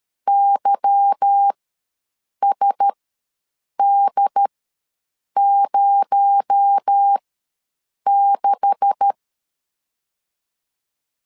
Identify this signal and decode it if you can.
Morse code, heard as a keyed tone, about 13 wpm, spaced as for 7 wpm, 800 Hz. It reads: YSD06
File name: LEZIONI / 10 SIGNS / 10A SLASH ws 12wpm 3443